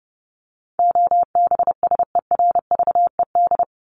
Morse code, heard as a keyed tone, about 30 wpm, 700 Hz. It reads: OBSERVED
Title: ED Suffix - Top 100 Words with Base Words 30wpm